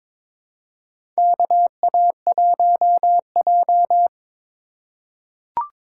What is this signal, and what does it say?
Morse code, heard as a keyed tone, about 22 wpm, 700 Hz. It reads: KA1J
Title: USA Call Signs - 6x Character Spacing - 22wpm